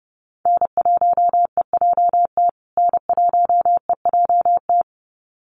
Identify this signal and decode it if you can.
Morse code, heard as a keyed tone, about 30 wpm, 700 Hz. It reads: N1EJT N1EJT
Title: USA Call Signs Repeated 2x 30wpm